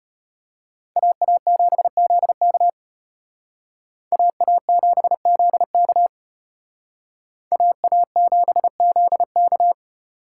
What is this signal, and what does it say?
Morse code, heard as a keyed tone, about 32 wpm, 700 Hz. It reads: AA7ZK AA7ZK AA7ZK